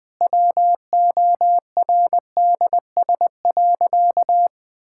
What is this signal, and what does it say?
Morse code, heard as a keyed tone, about 20 wpm, 700 Hz. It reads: WORDS.
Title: Sentences with 7-Word Phrases - Spoken Beginning - 20wpm